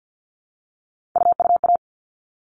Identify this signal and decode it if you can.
Morse code, heard as a keyed tone, about 50 wpm, 700 Hz. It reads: UUA
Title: All Cut Numbers - 3-Digit 50wpm